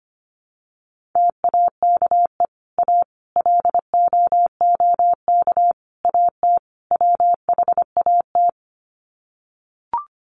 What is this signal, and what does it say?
Morse code, heard as a keyed tone, about 25 wpm, 700 Hz. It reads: TAKE A LOOK AT WHAT
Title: Sets of 5 Words Encore2 25wpm